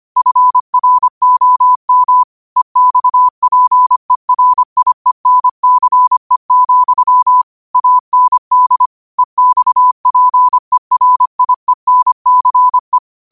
Morse code, heard as a keyed tone, about 25 wpm, 1000 Hz. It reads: FROM EXPERIENCE, AND EXPERIENCE